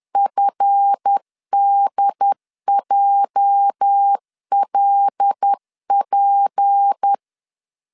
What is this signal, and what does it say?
Morse code, heard as a keyed tone, about 10 wpm, 800 Hz. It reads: FDJLP
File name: LEZIONI / 8 YJ45 / 8 YJ 10wpm 2651